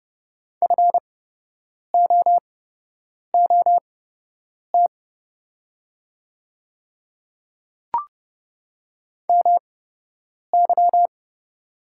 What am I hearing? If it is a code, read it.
Morse code, heard as a keyed tone, about 30 wpm, spaced as for 8 wpm, 700 Hz. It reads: FOOT MY
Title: 8x Character Spacing - Top 500 Words - 30wpm